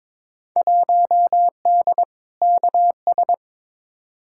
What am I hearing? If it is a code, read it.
Morse code, heard as a keyed tone, about 23 wpm, 700 Hz. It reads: 1D KS
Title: Field Day Exchanges - 22wpm